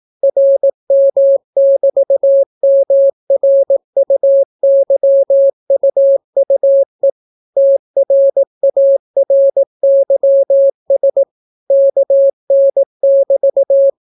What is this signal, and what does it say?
Morse code, heard as a keyed tone, about 18 wpm, 550 Hz. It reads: RM<BT>MRUYUUE TRARYS KN<BT>